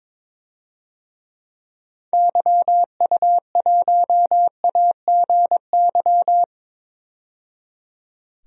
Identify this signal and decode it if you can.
Morse code, heard as a keyed tone, about 22 wpm, 700 Hz. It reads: YU1AGY
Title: DXCC-100 Call Signs - 6 Characters - 22wpm